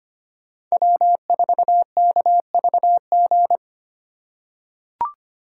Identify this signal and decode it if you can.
Morse code, heard as a keyed tone, about 25 wpm, 700 Hz. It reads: W4KVG
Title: USA Call Signs - 5 Characters - 4x Character Spacing - 25wpm